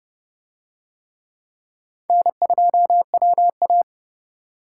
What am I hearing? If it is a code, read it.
Morse code, heard as a keyed tone, about 30 wpm, 700 Hz. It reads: N2WA